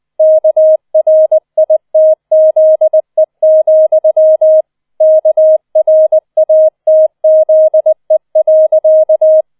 Morse code, heard as a keyed tone, about 20 wpm, 630 Hz. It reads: KRITZE, KRATZE.